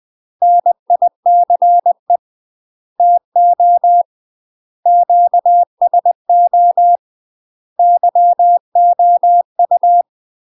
Morse code, heard as a keyed tone, about 20 wpm, 700 Hz. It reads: NICE TO QSO YOU
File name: UK MASTER QSOs TRAINING / 20WPM / UK MASTER QSOs 0044